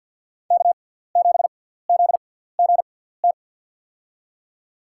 Morse code, heard as a keyed tone, about 50 wpm, spaced as for 17 wpm, 700 Hz. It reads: K7ZGT
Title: USA Call Signs - 5 Characters - 6x Character Spacing - 50wpm